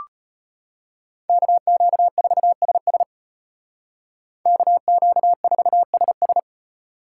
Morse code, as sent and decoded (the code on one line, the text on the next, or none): -.- --.- ....- ... ... / -.- --.- ....- ... ...
KQ4SS KQ4SS